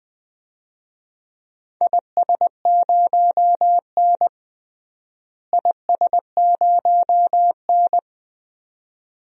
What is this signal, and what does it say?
Morse code, heard as a keyed tone, about 20 wpm, 700 Hz. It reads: IS0N IS0N
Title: DXCC-100 Call Signs - 4 Characters - Repeated 2x - 20wpm